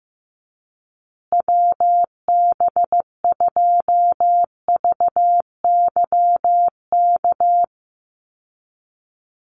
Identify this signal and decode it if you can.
Morse code, heard as a keyed tone, about 15 wpm, 700 Hz. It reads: WB2VYK
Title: Call Signs Repeated 2x - 15wpm